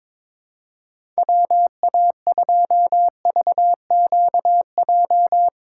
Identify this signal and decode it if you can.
Morse code, heard as a keyed tone, about 21 wpm, 700 Hz. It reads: WA2VQJ
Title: USA Call Signs - 6 Characters - 2x Character Spacing - 22wpm